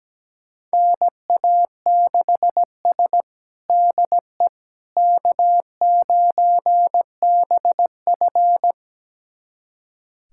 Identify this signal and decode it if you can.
Morse code, heard as a keyed tone, about 17 wpm, 700 Hz. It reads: NA6S DE K9BF
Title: USA Callsign DE Callsign - 4 Characters - 17wpm